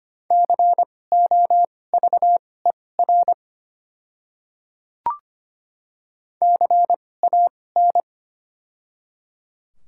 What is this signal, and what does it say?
Morse code, heard as a keyed tone, about 25 wpm, spaced as for 18 wpm, 700 Hz. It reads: COVER CAN
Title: 2x Character Spacing - Top 500 Words - 25wpm